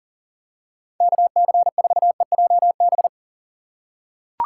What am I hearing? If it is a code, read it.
Morse code, heard as a keyed tone, about 40 wpm, 700 Hz. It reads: KC4EJB
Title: USA Call Signs - 6 Characters - 2x Character Spacing - 40wpm